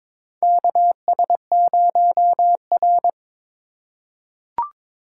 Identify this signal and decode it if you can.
Morse code, heard as a keyed tone, about 22 wpm, 700 Hz. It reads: KS0R
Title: USA Call Signs - 4 Characters - 4x Character Spacing - 22wpm